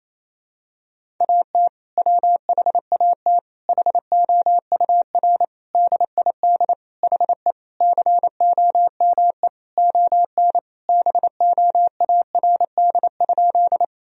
Morse code, heard as a keyed tone, about 28 wpm, 700 Hz. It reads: AT WHAT HOUR DID HE COME ON BOARD?
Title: Sentences from Top 1000 Words - Review Part 4 - 28wpm